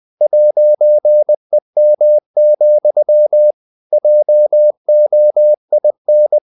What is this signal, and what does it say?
Morse code, heard as a keyed tone, about 20 wpm, 600 Hz. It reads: 'EM, JOIN
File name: CW / Practice / EnglishSayings20